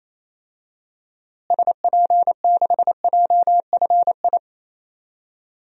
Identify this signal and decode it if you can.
Morse code, heard as a keyed tone, about 28 wpm, 700 Hz. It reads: SP6JFI
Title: DXCC-100 Call Signs - 6 Characters - Repeated 2x - 28wpm